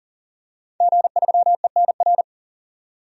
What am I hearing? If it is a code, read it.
Morse code, heard as a keyed tone, about 40 wpm, 700 Hz. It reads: G3ENR
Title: DXCC-100 Call Signs - 5 Characters - 40wpm